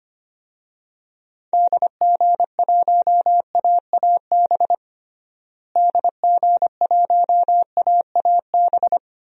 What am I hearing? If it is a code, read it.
Morse code, heard as a keyed tone, about 25 wpm, 700 Hz. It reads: DG1AAB DG1AAB
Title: DXCC-100 Call Signs - 6 Characters - Repeated 2x - 25wpm